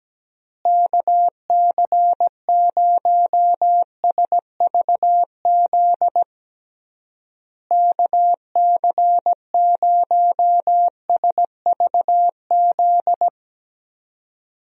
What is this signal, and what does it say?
Morse code, heard as a keyed tone, about 17 wpm, 700 Hz. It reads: KC0SVZ KC0SVZ
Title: USA Call Signs - 6 Characters - Repeated 3x - 17wpm